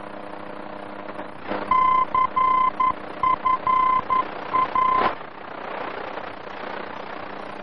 Morse code, heard as a keyed tone, about 11 wpm, 1030 Hz, with noise. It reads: CFA
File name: france / mx / CFA-VOR-F2